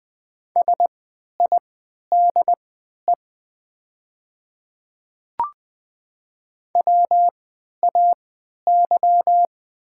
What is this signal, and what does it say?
Morse code, heard as a keyed tone, about 20 wpm, spaced as for 11 wpm, 700 Hz. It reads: SIDE WAY
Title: 3x Character Spacing - Top 200 Words - 20wpm